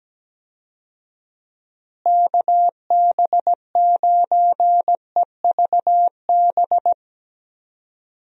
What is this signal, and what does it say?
Morse code, heard as a keyed tone, about 17 wpm, 700 Hz. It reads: KB9EVB
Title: USA Call Signs - 6 Characters - Repeated 3x - 17wpm